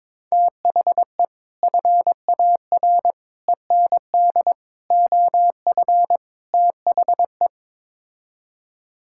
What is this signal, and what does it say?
Morse code, heard as a keyed tone, about 22 wpm, 700 Hz. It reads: THE FAR END OF THE
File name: practice / 22wpm / Build A Phrase - 5-Word Phrases - 22wpm